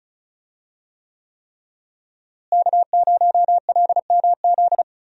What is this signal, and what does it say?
Morse code, heard as a keyed tone, about 36 wpm, 700 Hz. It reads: K0LMZ
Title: USA Call Signs - 5 Characters - Repeated 3x - 35wpm